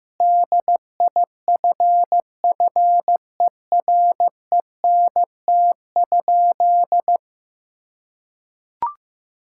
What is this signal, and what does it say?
Morse code, heard as a keyed tone, about 15 wpm, 700 Hz. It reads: DIFFERENT?
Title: Sentences from Top 1600 Words 15wpm